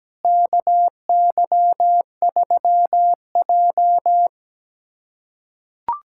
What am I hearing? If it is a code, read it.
Morse code, heard as a keyed tone, about 17 wpm, 700 Hz. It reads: KY3J
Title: USA Callsign DE Callsign - 4 Characters - 17wpm